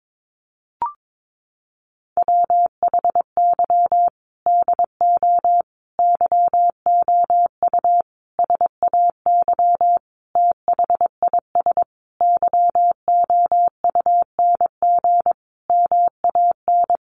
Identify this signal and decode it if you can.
Morse code, heard as a keyed tone, about 22 wpm, 700 Hz. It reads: WHY DO YOU SAY THIS YOUNG MAN
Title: Sentences from Top 1100 Words 22wpm